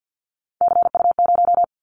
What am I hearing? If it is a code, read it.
Morse code, heard as a keyed tone, about 50 wpm, 700 Hz. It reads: CU0
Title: Callsign Prefixes for Top 100 DXCC - 50wpm